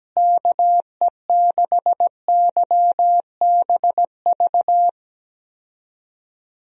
Morse code, heard as a keyed tone, about 17 wpm, 700 Hz. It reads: KE6YBV
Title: USA Call Signs - 6 Characters - Repeated 3x - 17wpm